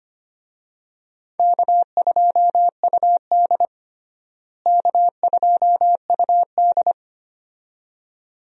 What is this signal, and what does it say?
Morse code, heard as a keyed tone, about 25 wpm, 700 Hz. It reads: K2UD K2UD
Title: USA Call Signs - 4 Characters - Repeated 2x - 25wpm